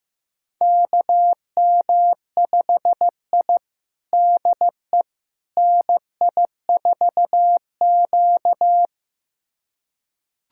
Morse code, heard as a keyed tone, about 15 wpm, 700 Hz. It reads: KM5I DE NI4Q
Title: USA Callsign DE Callsign - 4 Characters - 15wpm